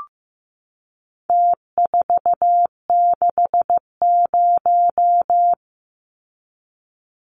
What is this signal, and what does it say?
Morse code, heard as a keyed tone, about 15 wpm, 700 Hz. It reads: T460
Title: Numbers - Contest Style - 4-Digit 15wpm